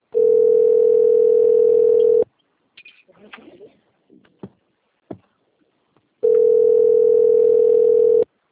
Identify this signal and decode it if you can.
Ringback tone (North America), 440 + 480 Hz.